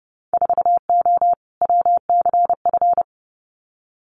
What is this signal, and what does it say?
Morse code, heard as a keyed tone, about 30 wpm, 700 Hz. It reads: 4O WCF
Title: Winter Field Day Exchanges Encore 30wpm